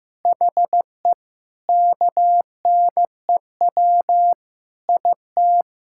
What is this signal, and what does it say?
Morse code, heard as a keyed tone, about 15 wpm, 700 Hz. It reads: HE KNEW IT
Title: Sentences from Top 1600 Words 15wpm